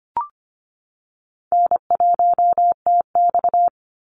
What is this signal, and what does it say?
Morse code, heard as a keyed tone, about 25 wpm, 700 Hz. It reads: N1TX
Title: Easy Call Signs - 25wpm